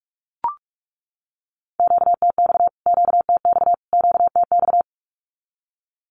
Morse code, heard as a keyed tone, about 44 wpm, 700 Hz. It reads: QTX QTX QTX
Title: Q-Codes Repeated 3x 45wpm